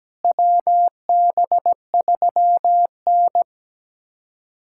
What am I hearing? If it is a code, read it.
Morse code, heard as a keyed tone, about 17 wpm, 700 Hz. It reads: WB3N
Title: USA Callsign DE Callsign - 4 Characters - 17wpm